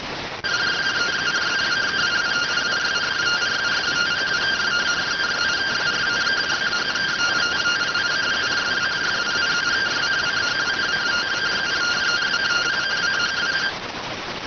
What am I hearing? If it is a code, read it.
Two-tone teleprinter signal, about 170 Hz shift.